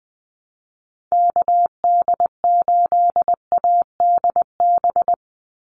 Morse code, heard as a keyed tone, about 20 wpm, 700 Hz. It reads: KD8ADB
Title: USA Call Signs 20wpm